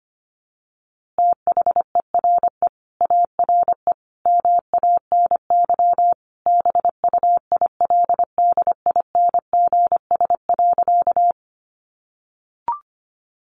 Morse code, heard as a keyed tone, about 25 wpm, 700 Hz. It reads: THERE ARE MANY BUILDINGS.
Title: Easy Phrases 25wpm